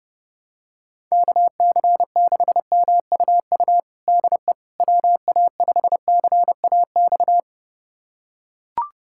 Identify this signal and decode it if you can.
Morse code, heard as a keyed tone, about 30 wpm, 700 Hz. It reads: KC6MUU DE WA5CAX
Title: USA Callsign DE Callsign - 6 Characters - 30wpm